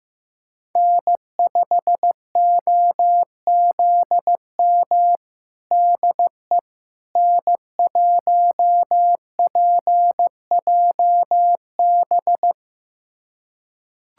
Morse code, heard as a keyed tone, about 16 wpm, 700 Hz. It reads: N5OZM DE N1PJB
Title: USA Callsign DE Callsign - 5 Characters - 15wpm